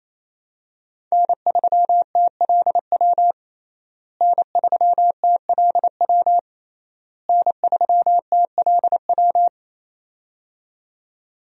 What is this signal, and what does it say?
Morse code, heard as a keyed tone, about 28 wpm, 700 Hz. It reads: N3TLW N3TLW N3TLW
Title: USA Call Signs - 5 Characters - Repeated 3x - 28wpm